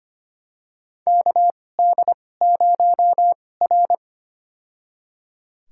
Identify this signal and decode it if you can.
Morse code, heard as a keyed tone, about 25 wpm, spaced as for 18 wpm, 700 Hz. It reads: KD0R
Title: USA Call Signs - 4 Characters - 2x Character Spacing - 25wpm